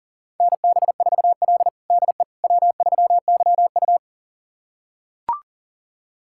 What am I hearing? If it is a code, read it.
Morse code, heard as a keyed tone, about 40 wpm, 700 Hz. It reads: NB4L DE W3YU
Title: USA Callsign DE Callsign - 4 Characters - 40wpm